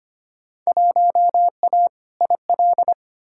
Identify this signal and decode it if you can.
Morse code, heard as a keyed tone, about 25 wpm, 700 Hz. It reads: 1A IL
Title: Field Day Exchanges - 25wpm